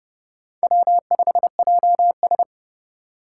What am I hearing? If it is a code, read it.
Morse code, heard as a keyed tone, about 30 wpm, 700 Hz. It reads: W5JS